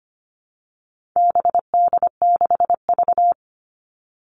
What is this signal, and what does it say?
Morse code, heard as a keyed tone, about 25 wpm, 700 Hz. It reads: BD6V
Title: All Cut Numbers - 4-Digit 25wpm